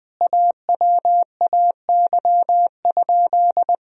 Morse code, heard as a keyed tone, about 21 wpm, 700 Hz. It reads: AWAY?
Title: Sentences from Top 1000 Words - Review Part 1 - 20wpm